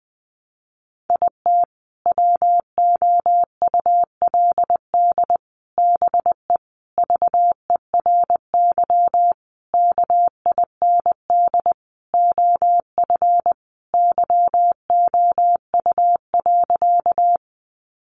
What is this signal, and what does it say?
Morse code, heard as a keyed tone, about 20 wpm, 700 Hz. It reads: IT WOULD BE VERY KIND OF YOU.
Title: Sentences from Top 200 Words 20wpm